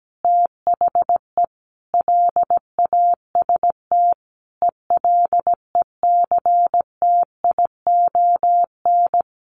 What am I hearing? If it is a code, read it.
Morse code, heard as a keyed tone, about 17 wpm, 700 Hz. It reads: THE LAST ELECTION
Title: Sentences from Top 1500 Words 17wpm